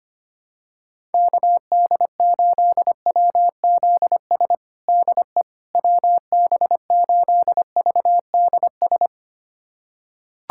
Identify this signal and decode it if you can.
Morse code, heard as a keyed tone, about 25 wpm, 700 Hz. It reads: KD8WZS DE WB8VDS
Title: USA Callsign DE Callsign - 6 Characters - 25wpm